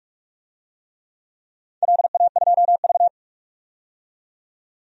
Morse code, heard as a keyed tone, about 46 wpm, 700 Hz. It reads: LA2V
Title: DXCC-100 Call Signs - 4 Characters - 45wpm